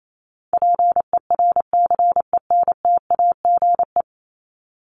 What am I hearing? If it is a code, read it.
Morse code, heard as a keyed tone, about 28 wpm, 700 Hz. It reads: PERCENTAGE
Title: Top 1001-1250 Words 28wpm